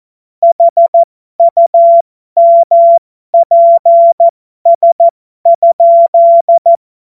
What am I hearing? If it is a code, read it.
Morse code, heard as a keyed tone, about 14 wpm, spaced as for 12 wpm, 680 Hz. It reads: HUMPS?
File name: Morsekurs / Witze